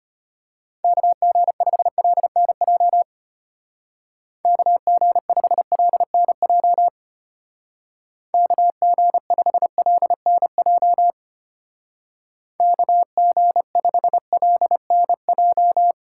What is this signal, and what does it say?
Morse code, heard as a keyed tone, about 31 wpm, 700 Hz. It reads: KG5LNJ KG5LNJ KG5LNJ KG5LNJ
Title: Speed Racing - USA Call Signs 25wpm